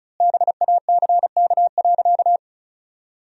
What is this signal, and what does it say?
Morse code, heard as a keyed tone, about 36 wpm, 700 Hz. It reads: BACK.
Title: 3x Word-Spacing - Sentences from Top 300 Words - 35wpm